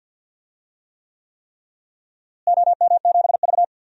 Morse code, heard as a keyed tone, about 52 wpm, 700 Hz. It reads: OM7V
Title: DXCC-100 Call Signs - 4 Characters - Repeated 3x - 50wpm